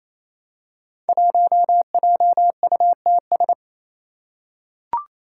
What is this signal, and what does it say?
Morse code, heard as a keyed tone, about 28 wpm, 700 Hz. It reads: 1JUTS
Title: Five Challenging Letter-Number Combinations 28wpm